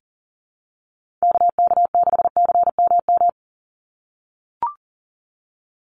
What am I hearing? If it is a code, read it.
Morse code, heard as a keyed tone, about 40 wpm, 700 Hz. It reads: KK6CMM
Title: USA Call Signs 40wpm